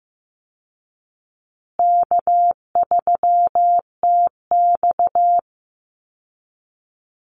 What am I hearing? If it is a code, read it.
Morse code, heard as a keyed tone, about 15 wpm, 700 Hz. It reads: K3TX
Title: Easy Call Signs Repeated 3x - 15wpm